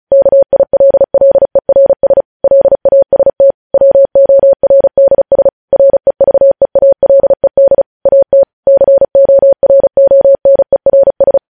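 Morse code, heard as a keyed tone, about 36 wpm, 550 Hz. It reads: KILLERS LAST WORDS REVEALED AT CORONERS
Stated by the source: Morse code news headlines at 35.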